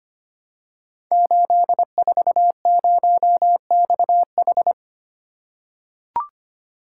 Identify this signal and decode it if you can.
Morse code, heard as a keyed tone, about 25 wpm, 700 Hz. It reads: 840XH
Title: Code Groups - 5 Letter:Numbers - 25wpm